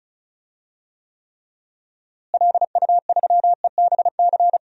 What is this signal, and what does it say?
Morse code, heard as a keyed tone, about 36 wpm, 700 Hz. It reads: LU3EBC
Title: DXCC-100 Call Signs - 6 Characters - Repeated 3x - 35wpm